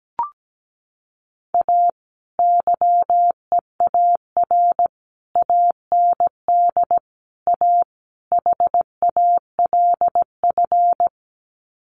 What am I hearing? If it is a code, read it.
Morse code, heard as a keyed tone, about 17 wpm, 700 Hz. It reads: A YEAR AND A HALF
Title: Sets of 5 Words 17wpm